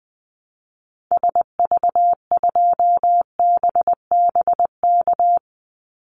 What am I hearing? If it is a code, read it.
Morse code, heard as a keyed tone, about 20 wpm, 700 Hz. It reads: SV2BBK
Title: Call Signs Repeated 2x - 20wpm